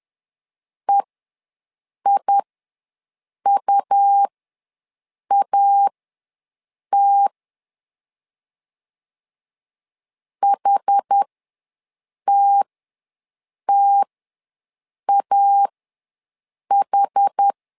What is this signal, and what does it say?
Morse code, heard as a keyed tone, about 10 wpm, spaced as for 6 wpm, 800 Hz. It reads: EIUAT HTTAH
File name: LEZIONI / 2 UHDN / 2 UH ws 10wpm 2029